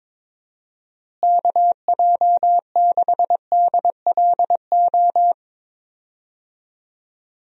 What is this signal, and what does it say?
Morse code, heard as a keyed tone, about 22 wpm, 700 Hz. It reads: KJ6DLO
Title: USA Call Signs - 6 Characters - Repeated 3x - 22wpm